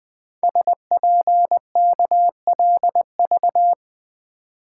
Morse code, heard as a keyed tone, about 20 wpm, 700 Hz. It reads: SPKLV
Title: Code Groups - 5 Letters - 20wpm